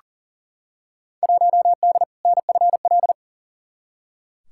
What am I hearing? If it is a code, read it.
Morse code, heard as a keyed tone, about 40 wpm, 700 Hz. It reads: 1D NFL
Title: Field Day Exchanges - 40wpm